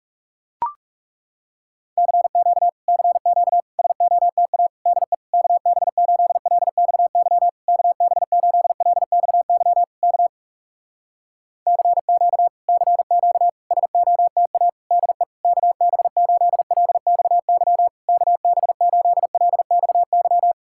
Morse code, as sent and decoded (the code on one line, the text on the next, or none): -.-. --.- / -.-. --.- / ... --- - .- / -.. / -.- -... ---.. .-.. -..- -.-- / -.- -... ---.. .-.. -..- -.-- / -.- / -.-. --.- / -.-. --.- / ... --- - .- / -.. / -.- -... ---.. .-.. -..- -.-- / -.- -... ---.. .-.. -..- -.--
CQ CQ SOTA DE KB8LXY KB8LXY K CQ CQ SOTA DE KB8LXY KB8LXY